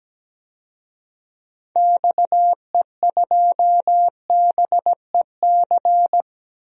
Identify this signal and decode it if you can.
Morse code, heard as a keyed tone, about 17 wpm, 700 Hz. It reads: XE2BEC
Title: DXCC-100 Call Signs - 6 Characters - 17wpm